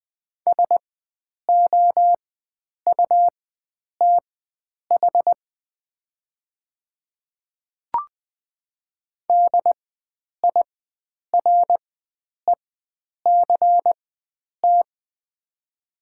Morse code, heard as a keyed tone, about 20 wpm, spaced as for 9 wpm, 700 Hz. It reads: SOUTH DIRECT
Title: 4x Character Spacing - Top 500 Words - 20wpm